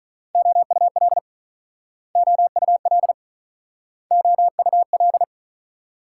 Morse code, heard as a keyed tone, about 43 wpm, 690 Hz. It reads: OUL OUL OUL
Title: Speed-Racing Multi-letter Phonemes - 35wpm